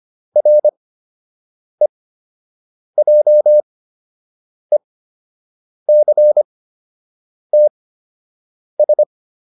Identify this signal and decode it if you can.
Morse code, heard as a keyed tone, about 25 wpm, spaced as for 7 wpm, 600 Hz. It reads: REJECTS